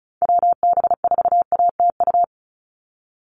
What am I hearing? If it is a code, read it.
Morse code, heard as a keyed tone, about 36 wpm, 700 Hz. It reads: WB4ATU
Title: USA Call Signs 35wpm